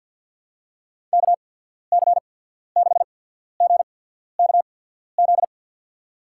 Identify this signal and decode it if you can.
Morse code, heard as a keyed tone, about 54 wpm, spaced as for 14 wpm, 700 Hz. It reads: KC6GKZ
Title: USA Call Signs - 6 Characters - 8x Character Spacing - 50wpm